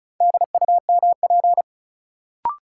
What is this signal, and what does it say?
Morse code, heard as a keyed tone, about 36 wpm, 700 Hz. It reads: DUMP
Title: Sentences from Top 1300 Words - New Word Drill - 35wpm